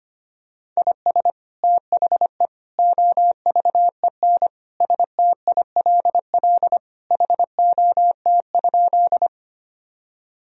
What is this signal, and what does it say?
Morse code, heard as a keyed tone, about 25 wpm, 700 Hz. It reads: IS THE OVEN STILL HOT?
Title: Sentences from Top 1000 Words - Review Part 4 - 25wpm